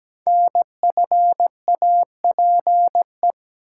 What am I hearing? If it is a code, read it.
Morse code, heard as a keyed tone, about 17 wpm, 700 Hz. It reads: NFAPE
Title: Code Groups - 5 Letters - 17wpm